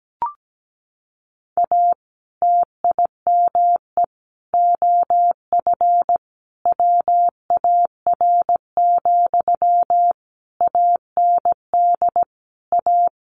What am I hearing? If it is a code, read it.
Morse code, heard as a keyed tone, about 17 wpm, 700 Hz. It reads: A TIME OF WAR, AND A
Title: Sentences from Top 1500 Words 17wpm